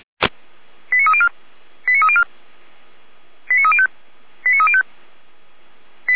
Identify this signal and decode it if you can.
Five-tone selective call (ZVEI), heard as quick runs of five tones, tones of about 70 ms. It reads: 89304 89304 89305 89305